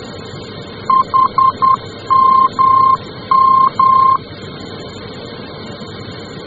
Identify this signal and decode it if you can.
Morse code, heard as a keyed tone, about 10 wpm, 1060 Hz. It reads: HMM
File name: germany / mx / HMM-D